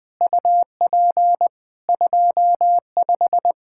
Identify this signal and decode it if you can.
Morse code, heard as a keyed tone, about 21 wpm, 700 Hz. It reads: UP 25
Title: Practice QSO - Exchange 2 - Info - 20wpm